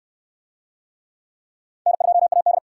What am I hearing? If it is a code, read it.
Morse code, heard as a keyed tone, about 63 wpm, 700 Hz. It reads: N2AD